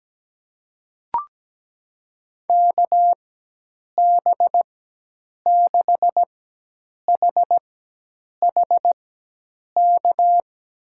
Morse code, heard as a keyed tone, about 17 wpm, spaced as for 8 wpm, 700 Hz. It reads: KB6HHK
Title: USA Call Signs - 6 Characters - 4x Character Spacing - 17wpm